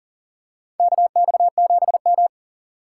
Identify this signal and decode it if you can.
Morse code, heard as a keyed tone, about 40 wpm, 690 Hz. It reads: KX7M
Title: K1USN SST QSO Practice - 40wpm